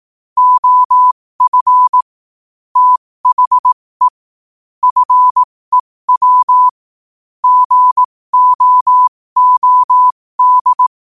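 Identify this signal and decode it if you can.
Morse code, heard as a keyed tone, about 18 wpm, spaced as for 15 wpm, 1000 Hz. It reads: OF THE FEW GOOD
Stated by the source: Morse quote-of-the-day practice at 15 WPM from Morse Resource.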